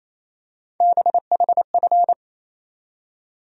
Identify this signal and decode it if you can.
Morse code, heard as a keyed tone, about 28 wpm, 700 Hz. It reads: BHF
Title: USA Call Sign Postfixes - 3 Letters 28wpm